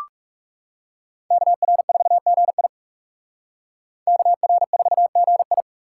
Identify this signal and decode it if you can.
Morse code, heard as a keyed tone, about 35 wpm, 700 Hz. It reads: KR4GI KR4GI
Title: Speed Racing - USA 2x2 Call Signs 30wpm